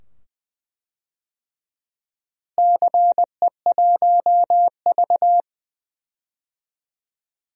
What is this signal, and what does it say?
Morse code, heard as a keyed tone, about 21 wpm, 700 Hz. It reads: CE1V
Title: DXCC-100 Call Signs - 4 Characters - Repeated 2x - 20wpm